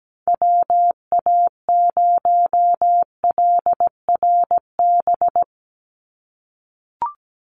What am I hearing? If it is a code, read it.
Morse code, heard as a keyed tone, about 17 wpm, 700 Hz. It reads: WA0LRB
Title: USA Call Signs 17wpm